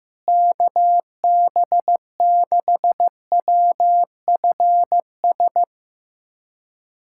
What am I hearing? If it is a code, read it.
Morse code, heard as a keyed tone, about 15 wpm, 700 Hz. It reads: KB6WFS
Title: USA Callsign DE Callsign - 6 Characters - 15wpm